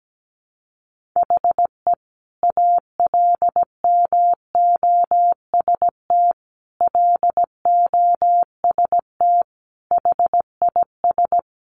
Morse code, heard as a keyed tone, about 17 wpm, 700 Hz. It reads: HE ALMOST LOST HIS
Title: Sentences from Top 1400 Words 17wpm